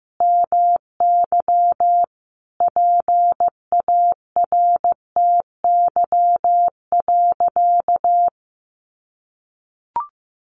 Morse code, heard as a keyed tone, about 14 wpm, 700 Hz. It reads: MY PARTY.
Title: Sentences from Top 1400 Words 15wpm